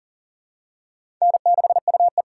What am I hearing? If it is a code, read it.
Morse code, heard as a keyed tone, about 40 wpm, 700 Hz. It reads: N6UE
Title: USA Call Signs - 6x Character Spacing - 40wpm